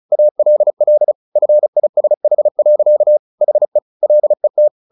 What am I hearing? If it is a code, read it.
Morse code, heard as a keyed tone, about 34 wpm, 600 Hz. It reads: ALL FISH. HE LET
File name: the old man and the sea 2 35wpm